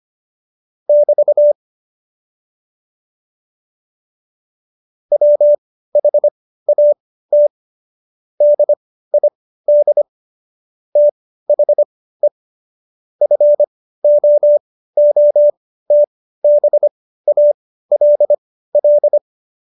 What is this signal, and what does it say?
Morse code, heard as a keyed tone, about 25 wpm, spaced as for 15 wpm, 590 Hz. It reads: <BT> WHAT DID THE FOOTBALL
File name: Jokes-25-char-15-eff